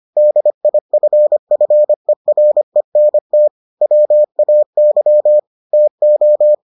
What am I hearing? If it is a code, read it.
Morse code, heard as a keyed tone, about 25 wpm, 600 Hz. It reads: DIFFERENT WAY TO